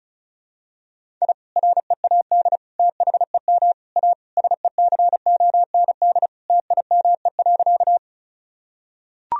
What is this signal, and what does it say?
Morse code, heard as a keyed tone, about 36 wpm, 700 Hz. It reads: I READ THEM A SECOND TIME.
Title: Sentences from Top 1000 Words - Review Part 1 - 35wpm